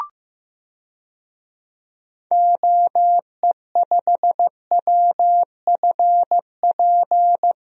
Morse code, heard as a keyed tone, about 16 wpm, 700 Hz. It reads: OE5WFP
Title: DXCC-100 Call Signs - 6 Characters - Repeated 2x - 15wpm